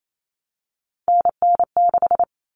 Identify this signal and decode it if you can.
Morse code, heard as a keyed tone, about 28 wpm, 700 Hz. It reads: NN6
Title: Common Cut Numbers - 3-Digit 28wpm